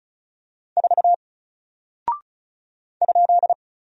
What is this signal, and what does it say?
Morse code, heard as a keyed tone, about 36 wpm, 700 Hz. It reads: 4 ?
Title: Single Character (Letter:Number:Punctuation:Prosign) Alt 35wpm